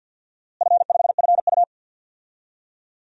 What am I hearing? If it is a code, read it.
Morse code, heard as a keyed tone, about 48 wpm, 700 Hz. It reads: F5FU
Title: DXCC-100 Call Signs - 4 Characters - Repeated 2x - 50wpm